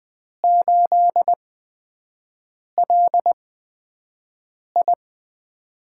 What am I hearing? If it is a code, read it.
Morse code, heard as a keyed tone, about 20 wpm, spaced as for 5 wpm, 700 Hz. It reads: 8LI
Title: USA Call Signs - 8x Character Spacing - 20wpm